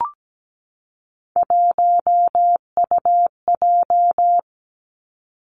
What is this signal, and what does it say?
Morse code, heard as a keyed tone, about 17 wpm, 700 Hz. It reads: 1UJ
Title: Three Challenging Letter-Number Combinations 17wpm